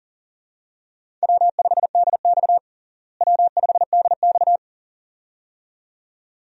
Morse code, heard as a keyed tone, about 37 wpm, 700 Hz. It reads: W5DX W5DX
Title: USA Call Signs - 4 Characters - Repeated 2x - 40wpm